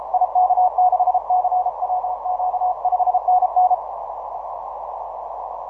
Morse code, heard as a keyed tone, about 32 wpm, 730 Hz. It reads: IK6CWQ 5NN
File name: soundclips / E31A 6MCW